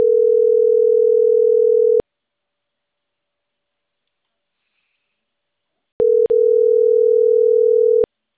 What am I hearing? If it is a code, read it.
Ringback tone (North America), 440 + 480 Hz.